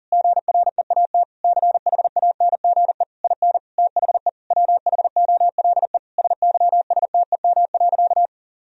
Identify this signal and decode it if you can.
Morse code, heard as a keyed tone, about 40 wpm, 700 Hz. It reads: GREAT CHANGE IN THE WHOLE SYSTEM.
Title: Sentences from Top 1000 Words - Review Part 3 - 40wpm